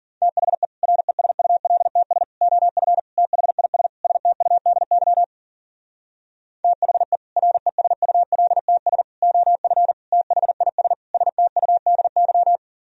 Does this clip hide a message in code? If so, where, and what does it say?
Yes: Morse code from 0.2 to 12.6 s, heard as a keyed tone, about 51 wpm, 700 Hz. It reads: THE RESULTS OF THIS STUDY THE RESULTS OF THIS STUDY